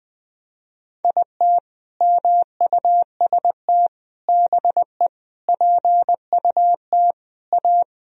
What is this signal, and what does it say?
Morse code, heard as a keyed tone, about 21 wpm, 700 Hz. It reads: IT MUST BE PUT A
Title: Sentences from Top 1000 Words - Review Part 1 - 20wpm